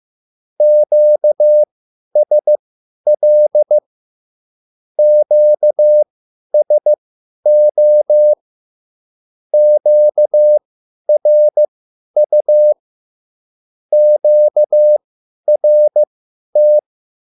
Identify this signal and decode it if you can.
Morse code, heard as a keyed tone, about 15 wpm, spaced as for 10 wpm, 600 Hz. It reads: QSL QSO QRU QRT
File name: QSL / CW lesson